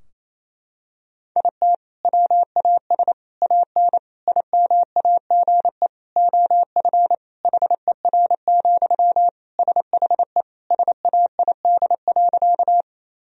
Morse code, heard as a keyed tone, about 28 wpm, 700 Hz. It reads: IT WAS AN IMAGE OF HER, SHE SAID.
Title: Sentences from Top 1000 Words - Review Part 4 - 28wpm